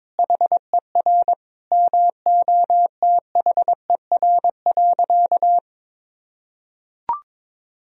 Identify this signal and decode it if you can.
Morse code, heard as a keyed tone, about 22 wpm, 700 Hz. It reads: HER MOTHER.
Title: Sentences from Top 1000 Words - Review Part 1 - 22wpm